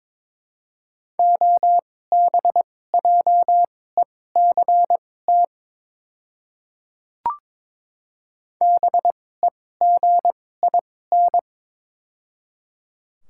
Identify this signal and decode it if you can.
Morse code, heard as a keyed tone, about 22 wpm, spaced as for 16 wpm, 700 Hz. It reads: OBJECT BEGIN
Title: 2x Character Spacing - Top 500 Words - 22wpm